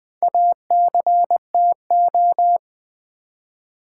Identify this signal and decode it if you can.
Morse code, heard as a keyed tone, about 20 wpm, 700 Hz. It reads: ACTO
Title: Build A Word - 1 Letter At A Time - 5-Letter Words - 20wpm